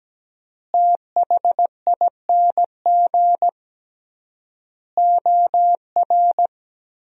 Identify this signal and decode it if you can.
Morse code, heard as a keyed tone, about 17 wpm, 700 Hz. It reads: THING OR
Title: 3x Word-Spacing - Sentences from Top 300 Words - 17wpm